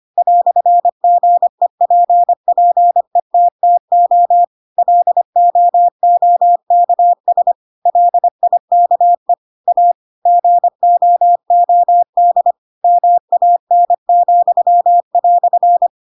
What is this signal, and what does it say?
Morse code, heard as a keyed tone, about 25 wpm, 700 Hz. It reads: "GEPPETTO LOOKS LIKE A GOOD MAN,"
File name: Pinocchio / Pinocchio ch3 25 25 ik1hge